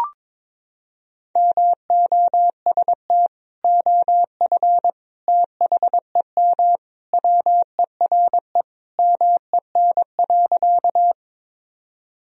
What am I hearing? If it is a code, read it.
Morse code, heard as a keyed tone, about 21 wpm, 700 Hz. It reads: MOST OF THEM WERE MEN.
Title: Sentences from Top 1000 Words - Review Part 1 - 22wpm